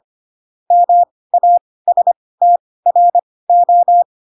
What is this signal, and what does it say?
Morse code, heard as a keyed tone, about 25 wpm, spaced as for 18 wpm, 700 Hz. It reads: MASTRO
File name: Pinocchio / Pinocchio ch2 12 25 ik1hge